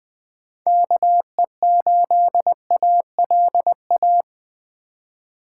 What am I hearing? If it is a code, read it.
Morse code, heard as a keyed tone, about 20 wpm, 700 Hz. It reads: KE8ALA
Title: Speed Racing - USA 2x3 Call Signs 17wpm